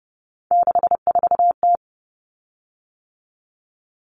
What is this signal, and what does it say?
Morse code, heard as a keyed tone, about 30 wpm, 700 Hz. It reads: 64T
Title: Common Cut Numbers - 3-Digit 30wpm